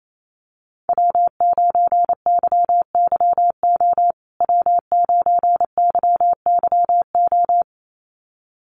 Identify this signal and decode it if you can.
Morse code, heard as a keyed tone, about 28 wpm, 700 Hz. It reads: W9YYO W9YYO
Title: USA Call Signs Repeated 2x 28wpm